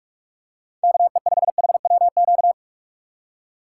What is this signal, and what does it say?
Morse code, heard as a keyed tone, about 47 wpm, 700 Hz. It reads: KE5HWQ